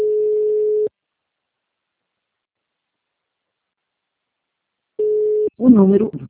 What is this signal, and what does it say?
Ringback tone (Europe, 425 Hz), with speech.
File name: out- -2007- - - .1294